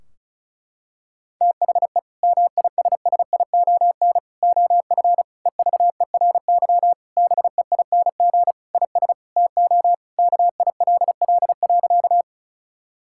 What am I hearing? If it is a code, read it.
Morse code, heard as a keyed tone, about 34 wpm, 700 Hz. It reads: THE MISSION OF EVERY BEING IS TO KILL.
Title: Sentences from Top 1600 Words 35wpm